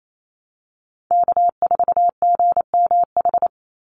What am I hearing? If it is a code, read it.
Morse code, heard as a keyed tone, about 28 wpm, 700 Hz. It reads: K4GMH
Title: Call Signs Repeated 2x - 28wpm